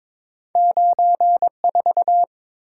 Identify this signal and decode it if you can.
Morse code, heard as a keyed tone, about 22 wpm, 700 Hz. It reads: 94
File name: practice / 22wpm / Two Letter-Number Combinations - 22wpm